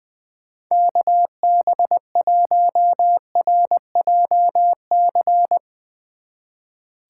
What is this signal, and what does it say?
Morse code, heard as a keyed tone, about 21 wpm, 700 Hz. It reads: KB1RJC
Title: 2023 SOTA Chaser Call Signs - 20wpm